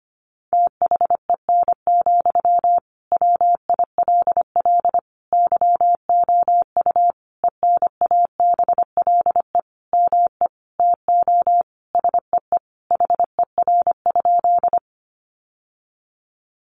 Morse code, heard as a keyed tone, about 25 wpm, 700 Hz. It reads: THEN, WILL YOU ENABLE ME TO SEE HER?
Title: Sentences from Top 1500 Words 25wpm